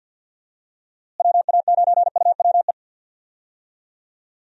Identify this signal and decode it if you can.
Morse code, heard as a keyed tone, about 49 wpm, 700 Hz. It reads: WA9UWE
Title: USA Call Signs - 6 Characters - Repeated 3x - 50wpm